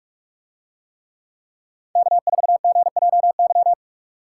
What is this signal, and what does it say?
Morse code, heard as a keyed tone, about 46 wpm, 700 Hz. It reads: K4GJY